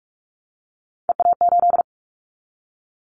Morse code, heard as a keyed tone, about 44 wpm, 700 Hz. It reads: EA8
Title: Callsign Prefixes for Top 100 DXCC - 45wpm